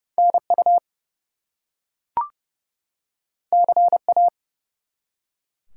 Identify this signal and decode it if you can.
Morse code, heard as a keyed tone, about 30 wpm, 700 Hz. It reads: NU CA
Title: US States and Canadian Provinces 30wpm